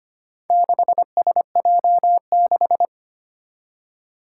Morse code, heard as a keyed tone, about 25 wpm, 700 Hz. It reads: 6SJ6
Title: Four Challenging Letter-Number Combinations 25wpm